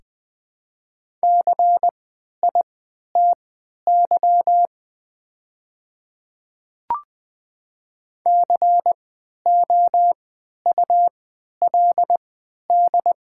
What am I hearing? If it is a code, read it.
Morse code, heard as a keyed tone, about 20 wpm, spaced as for 11 wpm, 700 Hz. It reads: CITY COULD
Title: 3x Character Spacing - Top 300 Words - 20wpm